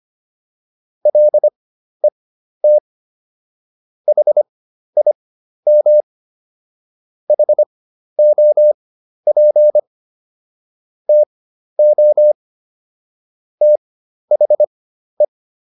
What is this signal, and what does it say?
Morse code, heard as a keyed tone, about 25 wpm, spaced as for 12 wpm, 600 Hz. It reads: LET HIM HOP TO THE